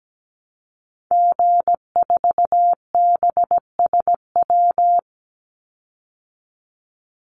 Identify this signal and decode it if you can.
Morse code, heard as a keyed tone, about 17 wpm, 700 Hz. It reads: G4BSW
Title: Call Signs Repeated 3x - 17wpm